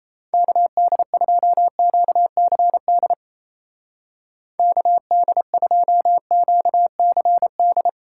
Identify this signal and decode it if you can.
Morse code, heard as a keyed tone, about 34 wpm, 700 Hz. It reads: KD2QCD KD2QCD
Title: Speed Racing - USA 2x3 Call Signs 28wpm